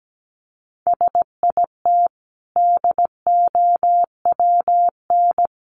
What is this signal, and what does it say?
Morse code, heard as a keyed tone, about 17 wpm, 700 Hz. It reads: SIT DOWN
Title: Sentences from Top 500 Words 17wpm